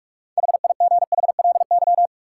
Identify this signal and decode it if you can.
Morse code, heard as a keyed tone, about 46 wpm, 700 Hz. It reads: HIGHLY
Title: Sentences from Top 1100 Words - New Word Drill Alt - 45wpm